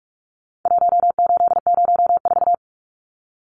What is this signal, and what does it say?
Morse code, heard as a keyed tone, about 46 wpm, 700 Hz. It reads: 1804
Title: Numbers - Contest Style - 4-Digit 45wpm